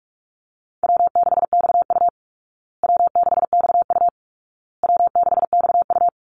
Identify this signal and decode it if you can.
Morse code, heard as a keyed tone, about 44 wpm, 700 Hz. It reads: W6XU W6XU W6XU
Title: Easy Call Signs Repeated 3x - 45wpm